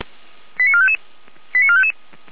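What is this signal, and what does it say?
Five-tone selective call (ZVEI), heard as quick runs of five tones, tones of about 70 ms. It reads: 89455 89455
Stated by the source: Received on SDR, ZVEI five-tone call.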